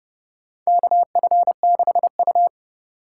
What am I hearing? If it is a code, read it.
Morse code, heard as a keyed tone, about 30 wpm, 700 Hz. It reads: KF6U
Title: USA Call Signs - 4 Characters - 2x Character Spacing - 30wpm